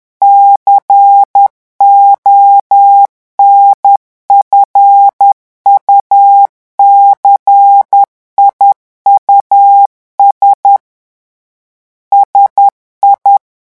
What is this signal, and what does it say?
Morse code, heard as a keyed tone, about 10 wpm, 800 Hz. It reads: CONFUCIUS SI
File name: morse mp3 / revision francais 10 mots minute